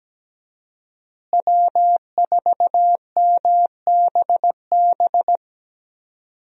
Morse code, heard as a keyed tone, about 17 wpm, 700 Hz. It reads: W4MBB
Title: USA Call Signs - 5 Characters - Repeated 2x - 17wpm